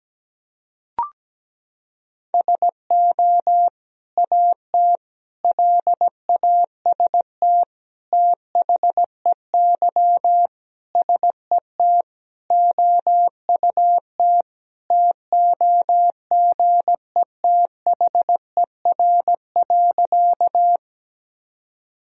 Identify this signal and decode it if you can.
Morse code, heard as a keyed tone, about 17 wpm, 700 Hz. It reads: SO AT LAST THEY SET OUT TOGETHER.
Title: Sentences from Top 1000 Words - Review Part 1 - 17wpm